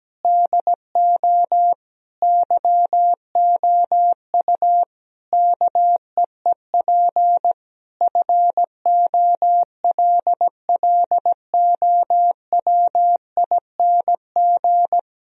Morse code, heard as a keyed tone, about 17 wpm, 700 Hz. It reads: DO YOU KEEP FOLLOWING
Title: Sentences from Top 1000 Words - Review Part 4 - 17wpm